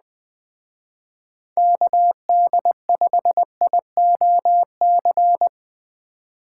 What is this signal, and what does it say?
Morse code, heard as a keyed tone, about 20 wpm, 700 Hz. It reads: KD5IOC